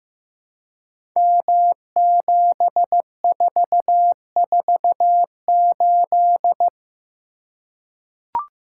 Morse code, heard as a keyed tone, about 15 wpm, 700 Hz. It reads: M7448
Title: Code Groups - 5 Letter:Numbers - 15wpm